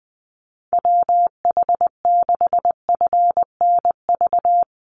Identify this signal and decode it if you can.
Morse code, heard as a keyed tone, about 20 wpm, 700 Hz. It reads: WH6FNV
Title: USA Call Signs 20wpm